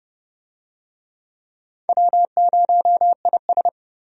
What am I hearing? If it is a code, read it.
Morse code, heard as a keyed tone, about 30 wpm, 700 Hz. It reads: W0IS
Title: USA Call Signs - 4 Characters - Repeated 3x - 30wpm